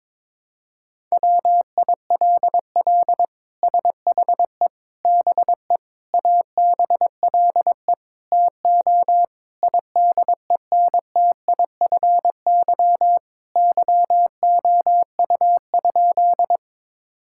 Morse code, heard as a keyed tone, about 21 wpm, 700 Hz. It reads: WILL SHE BE ABLE TO IDENTIFY YOU?
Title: Sentences from Top 1000 Words - Review Part 6 - 22wpm